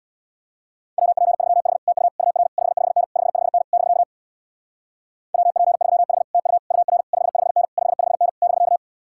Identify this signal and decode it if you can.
Morse code, heard as a keyed tone, about 63 wpm, 700 Hz. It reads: WP3H TU UR 55N 55N <BK> WP3H TU UR 55N 55N <BK>